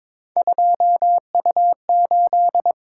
Morse code, heard as a keyed tone, about 22 wpm, 700 Hz. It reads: 2U8
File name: practice / 22wpm / Three Letter-Number Combinations - 22wpm